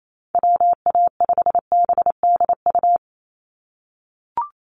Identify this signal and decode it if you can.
Morse code, heard as a keyed tone, about 28 wpm, 700 Hz. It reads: WA5BDU
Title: Call Signs - 28wpm